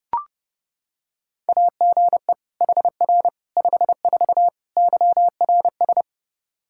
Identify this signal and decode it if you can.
Morse code, heard as a keyed tone, about 29 wpm, 700 Hz. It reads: AGE HR 54 YRS
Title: Practice QSO - Exchange 3 - More Info - 30wpm